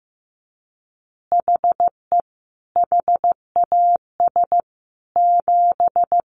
Morse code, heard as a keyed tone, about 15 wpm, 700 Hz. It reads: HE HAS 7
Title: Additional Easy Phrases 15wpm